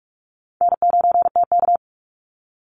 Morse code, heard as a keyed tone, about 44 wpm, 700 Hz. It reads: N9TK
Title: Easy Call Signs - 45wpm